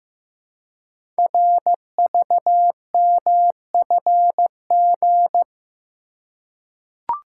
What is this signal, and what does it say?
Morse code, heard as a keyed tone, about 15 wpm, 700 Hz. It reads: RVMFG
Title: Code Groups - 5 Letter:Numbers - 15wpm